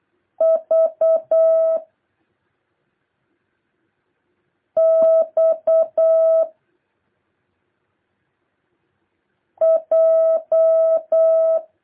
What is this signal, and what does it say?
Morse code, heard as a keyed tone, about 8 wpm, 650 Hz. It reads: V X J
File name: jh8cbh / 2023cw